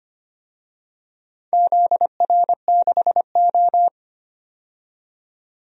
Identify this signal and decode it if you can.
Morse code, heard as a keyed tone, about 25 wpm, 700 Hz. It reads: ZR6O
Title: DXCC-100 Call Signs - 4 Characters - 25wpm